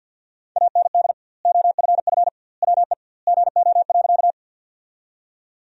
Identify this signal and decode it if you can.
Morse code, heard as a keyed tone, about 50 wpm, 700 Hz. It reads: AND OFF WE GO.
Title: 2x Word-Spacing - Sentences from Top 200 Words - 50wpm